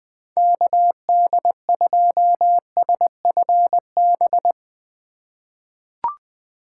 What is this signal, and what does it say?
Morse code, heard as a keyed tone, about 20 wpm, 700 Hz. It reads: KD2SFB
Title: USA Call Signs - 6 Characters - 4x Character Spacing - 20wpm